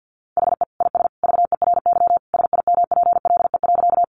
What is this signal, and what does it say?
Morse code, heard as a keyed tone, about 50 wpm, 700 Hz. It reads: HE IS VERY SIMPLE.
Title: Sentences from Top 400 Words 50wpm